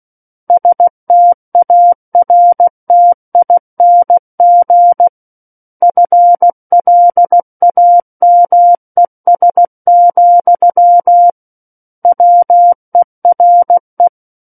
Morse code, heard as a keyed tone, about 16 wpm, 700 Hz. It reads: STARTING FLAMES, WERE
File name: Book One / 07-Chapter 7 - 16 WPM